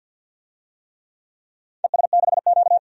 Morse code, heard as a keyed tone, about 52 wpm, 700 Hz. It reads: ES6Q
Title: DXCC-100 Call Signs - 4 Characters - 50wpm